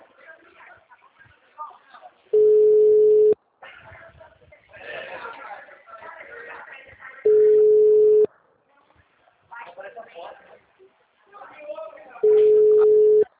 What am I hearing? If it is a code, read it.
Ringback tone (Europe, 425 Hz), with speech.
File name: out- -2002- - - .920